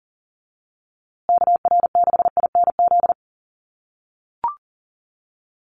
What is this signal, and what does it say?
Morse code, heard as a keyed tone, about 40 wpm, 700 Hz. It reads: KR6INZ
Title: USA Call Signs Repeated 2x 40wpm